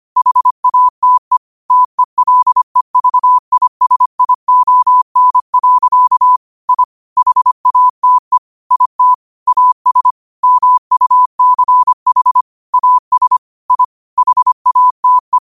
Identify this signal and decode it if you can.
Morse code, heard as a keyed tone, about 25 wpm, 1000 Hz. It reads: HATE TELEVISION. I HATE IT AS MUCH AS I HATE
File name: qotdfiles / 25wpm